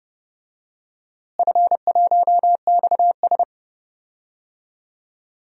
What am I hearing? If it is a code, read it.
Morse code, heard as a keyed tone, about 30 wpm, 700 Hz. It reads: F1XS
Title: DXCC-100 Call Signs - 4 Characters - Repeated 2x - 30wpm